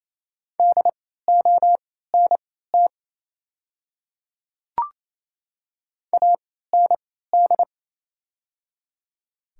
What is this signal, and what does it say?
Morse code, heard as a keyed tone, about 27 wpm, spaced as for 16 wpm, 700 Hz. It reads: DONT AND
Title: 3x Character Spacing - Top 300 Words - 28wpm